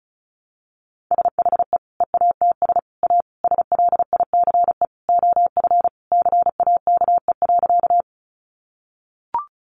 Morse code, heard as a keyed tone, about 34 wpm, 700 Hz. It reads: SHE EATS A SLICE OF CAKE.
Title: Easy Phrases 35wpm